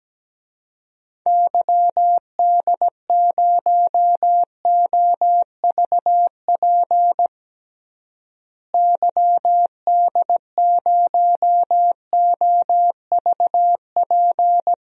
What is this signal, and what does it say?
Morse code, heard as a keyed tone, about 17 wpm, 700 Hz. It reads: YD0OVP YD0OVP
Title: DXCC-100 Call Signs - 6 Characters - Repeated 3x - 17wpm